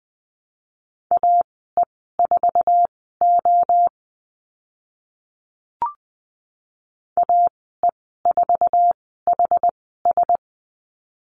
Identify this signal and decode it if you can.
Morse code, heard as a keyed tone, about 20 wpm, spaced as for 14 wpm, 700 Hz. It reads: AE4O AE4HS
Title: 062 Lesson 11g - Letter h - Call Signs 20wpm